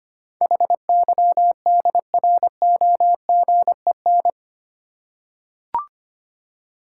Morse code, heard as a keyed tone, about 25 wpm, 700 Hz. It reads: HYDROGEN
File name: 8 Letter Words - First 3 Letters Given - 25wpm